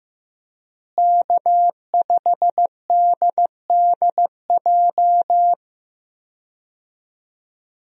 Morse code, heard as a keyed tone, about 15 wpm, 700 Hz. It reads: K5DDJ
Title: USA Call Signs - 5 Characters - Repeated 2x - 15wpm